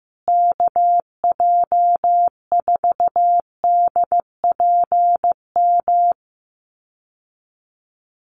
Morse code, heard as a keyed tone, about 15 wpm, 700 Hz. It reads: KJ4DPM
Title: USA Call Signs Repeated 2x 15wpm